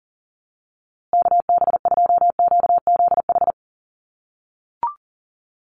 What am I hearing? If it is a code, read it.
Morse code, heard as a keyed tone, about 40 wpm, 700 Hz. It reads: KB2QZH
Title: USA Call Signs Repeated 2x 40wpm